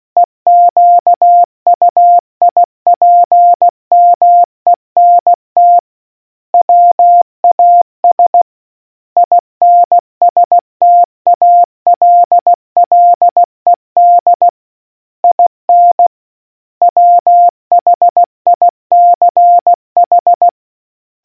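Morse code, heard as a keyed tone, about 16 wpm, 700 Hz. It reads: EQUIPMENT WAS INSTALLED IN WHICH
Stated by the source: Morse quiz question from the RNZN Communicators Association.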